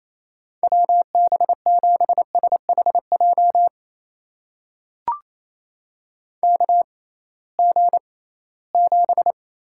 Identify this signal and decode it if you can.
Morse code, heard as a keyed tone, about 28 wpm, spaced as for 18 wpm, 700 Hz. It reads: WB7SHJ KG7
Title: USA Call Signs - 6 Characters - 6x Character Spacing - 28wpm